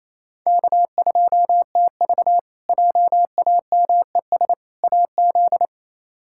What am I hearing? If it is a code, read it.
Morse code, heard as a keyed tone, about 27 wpm, 700 Hz. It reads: K2TV JAMES AZ
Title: K1USN SST QSO Practice - 28wpm